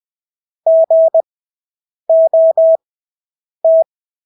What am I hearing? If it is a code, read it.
Morse code, heard as a keyed tone, about 20 wpm, spaced as for 8 wpm, 650 Hz. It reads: GOT
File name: Engelske ord 20/8 wpm